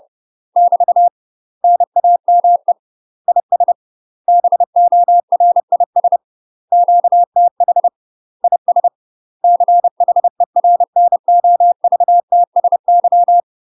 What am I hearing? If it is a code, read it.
Morse code, heard as a keyed tone, about 30 wpm, 700 Hz. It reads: <BT> NAME IS BORIS QTH IS CHERNOVTSY
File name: UK MASTER QSOs TRAINING / 30WPM / UK MASTER QSOs 0005